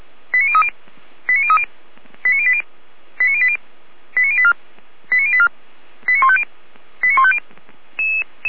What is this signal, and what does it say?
Five-tone selective call (ZVEI), heard as quick runs of five tones, tones of about 70 ms. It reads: 89030 89030 89080 89080 89084 89084 89150 89150
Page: SDR/ZVEI